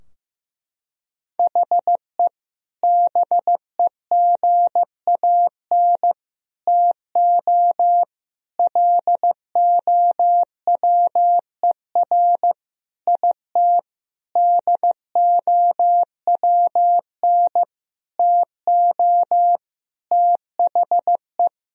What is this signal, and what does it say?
Morse code, heard as a keyed tone, about 15 wpm, 700 Hz. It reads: HE BEGAN TO LOWER IT DOWN TO THE
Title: Sentences from Top 1000 Words - Review Part 4 - 15wpm